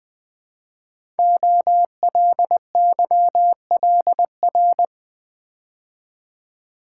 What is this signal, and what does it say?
Morse code, heard as a keyed tone, about 20 wpm, 700 Hz. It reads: OLYLR
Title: Code Groups - 5 Letter:Numbers - 20wpm